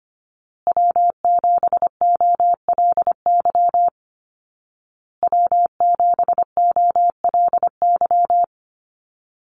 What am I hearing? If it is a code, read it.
Morse code, heard as a keyed tone, about 25 wpm, 700 Hz. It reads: W7OLY W7OLY
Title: Call Signs Repeated 2x - 25wpm